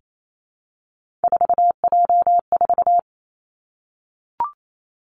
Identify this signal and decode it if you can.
Morse code, heard as a keyed tone, about 28 wpm, 700 Hz. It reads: 4J4
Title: Three Challenging Letter-Number Combinations 28wpm